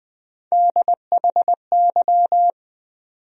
Morse code, heard as a keyed tone, about 21 wpm, 700 Hz. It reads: DHY
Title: USA Call Sign Postfixes - 3 Letters 20wpm